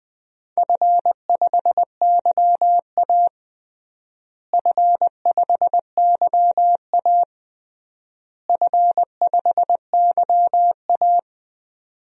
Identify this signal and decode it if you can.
Morse code, heard as a keyed tone, about 21 wpm, 700 Hz. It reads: F5YA F5YA F5YA
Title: DXCC-100 Call Signs - 4 Characters - Repeated 3x - 20wpm